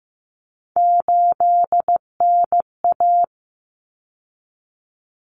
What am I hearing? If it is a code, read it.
Morse code, heard as a keyed tone, about 15 wpm, 700 Hz. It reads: 8NA
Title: Common Cut Numbers - 3-Digit 15wpm